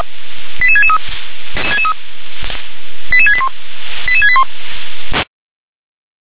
Five-tone selective call (ZVEI), heard as quick runs of five tones, tones of about 70 ms. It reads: 88663 88681 88681
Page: SDR/ZVEI